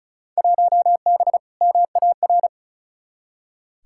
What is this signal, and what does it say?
Morse code, heard as a keyed tone, about 36 wpm, 700 Hz. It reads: 1B MAR
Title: Field Day Exchanges Encore - 35wpm